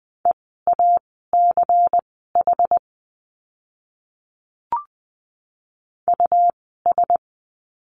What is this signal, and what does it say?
Morse code, heard as a keyed tone, about 20 wpm, spaced as for 14 wpm, 700 Hz. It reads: EACH US
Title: 230 Lesson 37f - Letter x - Words 20wpm